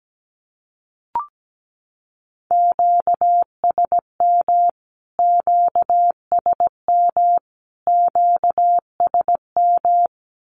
Morse code, heard as a keyed tone, about 17 wpm, 700 Hz. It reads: QSM QSM QSM
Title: Q-Codes Repeated 3x 17wpm